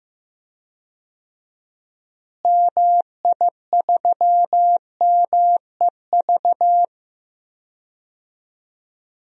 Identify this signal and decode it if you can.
Morse code, heard as a keyed tone, about 15 wpm, 700 Hz. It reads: MI3MEV
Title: DXCC-100 Call Signs - 6 Characters - Repeated 2x - 15wpm